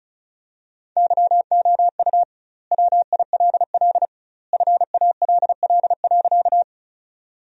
Morse code, heard as a keyed tone, about 36 wpm, 700 Hz. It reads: YOU WILL FALL.
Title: 2x Word-Spacing - Sentences from Top 500 Words - 35wpm